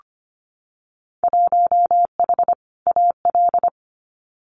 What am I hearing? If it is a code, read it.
Morse code, heard as a keyed tone, about 25 wpm, 700 Hz. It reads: 1H AL
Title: Winter Field Day Exchanges 25wpm